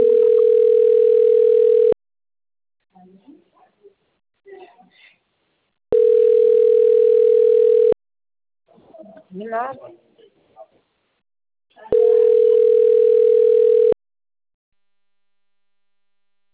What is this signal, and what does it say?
Ringback tone (North America), 440 + 480 Hz, with speech.
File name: out- - - - - .93099